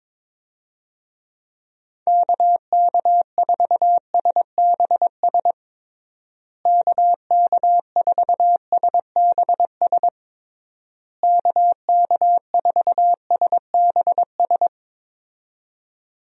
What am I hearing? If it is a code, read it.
Morse code, heard as a keyed tone, about 22 wpm, 700 Hz. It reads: KK4SBS KK4SBS KK4SBS
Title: USA Call Signs - 6 Characters - Repeated 3x - 22wpm